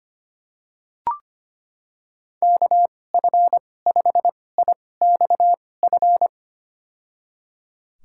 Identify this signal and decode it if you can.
Morse code, heard as a keyed tone, about 25 wpm, spaced as for 18 wpm, 700 Hz. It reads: KF5IXF
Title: USA Call Signs - 6 Characters - 2x Character Spacing - 25wpm